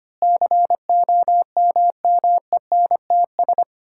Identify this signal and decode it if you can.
Morse code, heard as a keyed tone, about 25 wpm, 700 Hz. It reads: COMMENTS
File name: Sets of 9 Words - 25wpm